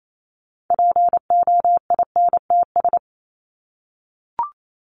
Morse code, heard as a keyed tone, about 28 wpm, 700 Hz. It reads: POINTS
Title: ES:S Suffix - Top 100 Words with Base Words 28wpm